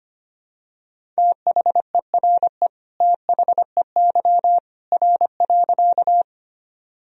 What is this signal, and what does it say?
Morse code, heard as a keyed tone, about 25 wpm, 700 Hz. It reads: THERE THEY R.
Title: Sentences from Top 100 Words and Abbreviations 25wpm